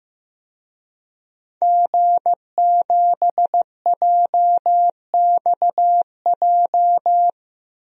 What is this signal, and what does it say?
Morse code, heard as a keyed tone, about 15 wpm, 700 Hz. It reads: G7JXJ
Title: DXCC-100 Call Signs - 5 Characters - 15wpm